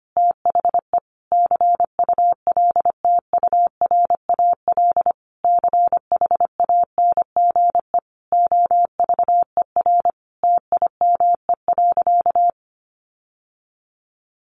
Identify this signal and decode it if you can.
Morse code, heard as a keyed tone, about 25 wpm, 700 Hz. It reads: THE CULTURAL CHANGE OVER TIME.
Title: Sentences from Top 1500 Words 25wpm